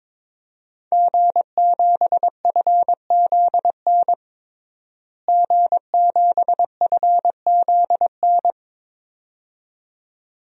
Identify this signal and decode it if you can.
Morse code, heard as a keyed tone, about 22 wpm, 700 Hz. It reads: G7FZN G7FZN
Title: DXCC-100 Call Signs - 5 Characters - Repeated 2x - 22wpm